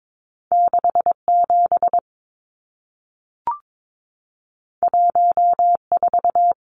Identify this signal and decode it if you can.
Morse code, heard as a keyed tone, about 22 wpm, 700 Hz. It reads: 67 14
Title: Numbers - 2-Digit 22wpm